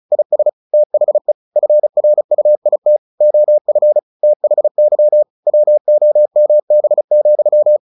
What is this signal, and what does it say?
Morse code, heard as a keyed tone, about 34 wpm, 600 Hz. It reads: IS THE FRUIT OF THY WOMB,
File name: the old man and the sea / the old man and the sea 1 35wpm